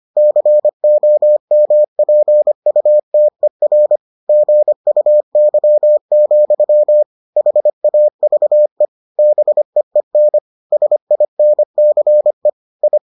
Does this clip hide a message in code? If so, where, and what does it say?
Yes: Morse code from 0.2 to 13.0 s, heard as a keyed tone, about 25 wpm, 600 Hz. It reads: COMPUTER GUY, HAVE BEEN SINCE I